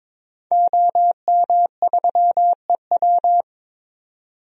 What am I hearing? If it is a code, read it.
Morse code, heard as a keyed tone, about 21 wpm, 700 Hz. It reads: OM3EW
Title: DXCC-100 Call Signs - 5 Characters - 22wpm